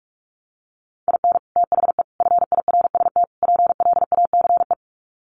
Morse code, heard as a keyed tone, about 44 wpm, 700 Hz. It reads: IN THE FIRST PLACE
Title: Sets of 4 Words 45wpm